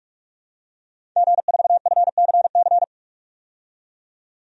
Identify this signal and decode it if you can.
Morse code, heard as a keyed tone, about 46 wpm, 700 Hz. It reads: G4FCC
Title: DXCC-100 Call Signs - 5 Characters - 45wpm